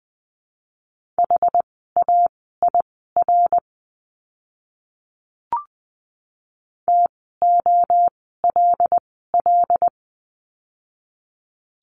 Morse code, heard as a keyed tone, about 20 wpm, spaced as for 15 wpm, 700 Hz. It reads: HAIR TOLL
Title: 075 Lesson 13f - Letter L - Words 20wpm